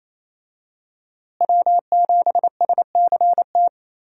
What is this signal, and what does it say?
Morse code, heard as a keyed tone, about 28 wpm, 700 Hz. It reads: W7SCT
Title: USA Call Signs - 5 Characters - Repeated 2x - 28wpm